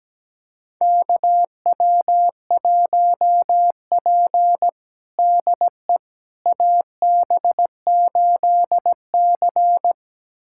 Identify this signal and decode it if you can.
Morse code, heard as a keyed tone, about 17 wpm, 700 Hz. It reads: KW1P DE AB8C
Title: USA Callsign DE Callsign - 4 Characters - 17wpm